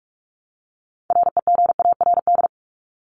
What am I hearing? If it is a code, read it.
Morse code, heard as a keyed tone, about 44 wpm, 700 Hz. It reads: REGARD
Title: ING Suffix - Top 100 Words with Base Words 45wpm